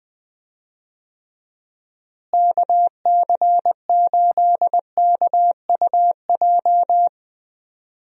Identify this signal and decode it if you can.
Morse code, heard as a keyed tone, about 20 wpm, 700 Hz. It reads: KC8KUJ
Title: USA Call Signs - 6 Characters - Repeated 3x - 20wpm